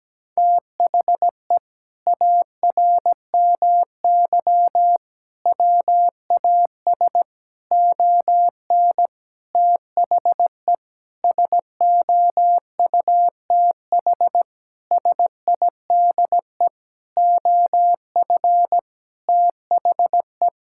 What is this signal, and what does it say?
Morse code, heard as a keyed tone, about 17 wpm, 700 Hz. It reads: THE ARMY WAS ON THE SOUTH SIDE OF THE
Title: Sentences from Top 1000 Words - Review Part 5 - 17wpm